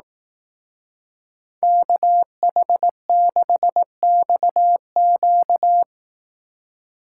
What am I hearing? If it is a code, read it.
Morse code, heard as a keyed tone, about 18 wpm, 700 Hz. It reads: KH6XQ
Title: Speed Racing - USA Call Signs 15wpm